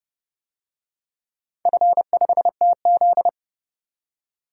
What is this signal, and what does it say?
Morse code, heard as a keyed tone, about 31 wpm, 700 Hz. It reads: F5TZ
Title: DXCC-100 Call Signs - 4 Characters - 30wpm